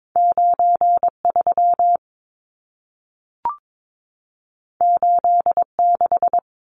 Morse code, heard as a keyed tone, about 22 wpm, 700 Hz. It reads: 93 86
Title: Numbers - 2-Digit 22wpm